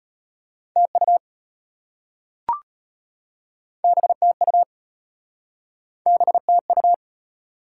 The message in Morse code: TU BTU BTU

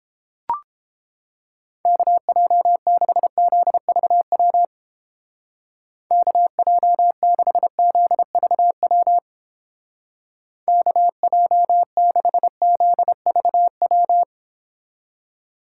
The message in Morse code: KJ6ZVW KJ6ZVW KJ6ZVW

